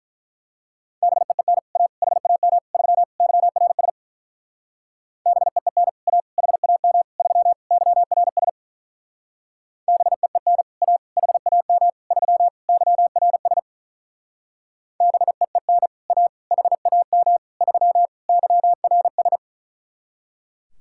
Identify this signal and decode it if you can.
Morse code, heard as a keyed tone, about 42 wpm, 700 Hz. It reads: BEEN A HAM 3 YRS BEEN A HAM 3 YRS BEEN A HAM 3 YRS BEEN A HAM 3 YRS
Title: Speed-Racing Common QSO Phrases - 35wpm